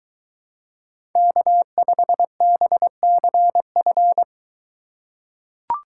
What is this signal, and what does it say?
Morse code, heard as a keyed tone, about 22 wpm, 700 Hz. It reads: K5BCF